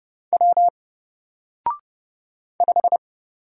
Morse code, heard as a keyed tone, about 31 wpm, 700 Hz. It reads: W 5
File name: Single-Letter-Number-30wpm